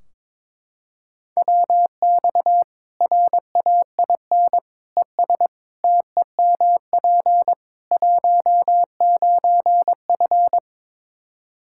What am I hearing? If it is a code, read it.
Morse code, heard as a keyed tone, about 22 wpm, 700 Hz. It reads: WX RAIN ES TEMP 19F
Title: Practice QSO - Exchange 2 - Info - 22wpm